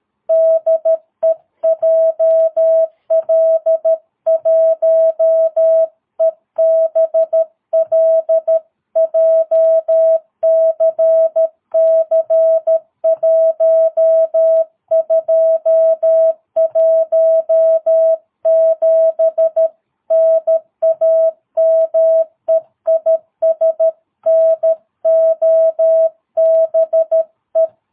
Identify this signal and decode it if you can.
Morse code, heard as a keyed tone, about 13 wpm, 660 Hz. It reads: DE JL1EBL JCC1217 NAME IS NOBE
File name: jh8cbh / 2022cw